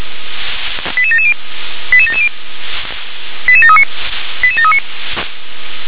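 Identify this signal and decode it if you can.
Five-tone selective call (ZVEI), heard as quick runs of five tones, tones of about 70 ms. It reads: 88770 88770 88739 88739
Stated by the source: ZVEI five-tone selective call 88739, received on an SDR.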